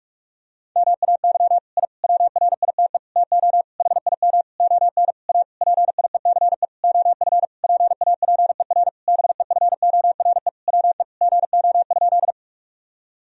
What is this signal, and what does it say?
Morse code, heard as a keyed tone, about 46 wpm, 700 Hz. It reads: MAY I WRITE TO HIM ON A PIECE OF PAPER BEFORE WE GO?
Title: Sentences from Top 1000 Words - Review Part 2 - 45wpm